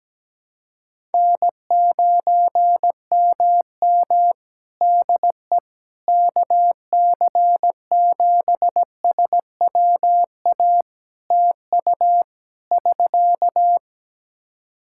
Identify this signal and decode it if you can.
Morse code, heard as a keyed tone, about 17 wpm, 700 Hz. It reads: N9MM DE KC7SWA TU <SK>
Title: Practice QSO - Exchange 4 - Ending - 17wpm